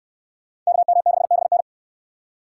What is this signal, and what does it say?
Morse code, heard as a keyed tone, about 63 wpm, 700 Hz. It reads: KM6BN